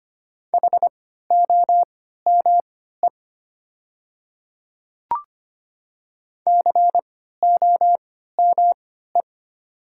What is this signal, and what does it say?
Morse code, heard as a keyed tone, about 25 wpm, spaced as for 14 wpm, 700 Hz. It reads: HOME COME
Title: 3x Character Spacing - Top 200 Words - 25wpm